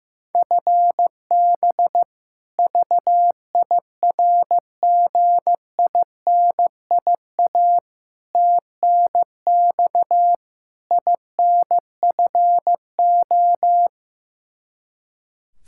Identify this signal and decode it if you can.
Morse code, heard as a keyed tone, about 15 wpm, 700 Hz. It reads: FB VIRGINIA TNX INFO
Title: K1USN SST QSO Practice 15wpm